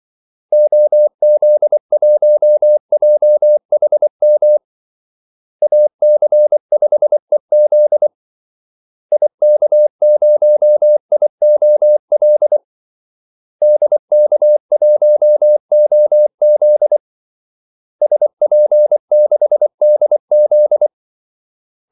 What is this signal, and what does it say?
Morse code, heard as a keyed tone, about 24 wpm, 600 Hz. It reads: OZ1JHM AC5EZ IK0IOL DK1OZ SP6DZ
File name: Callsigns-24wpm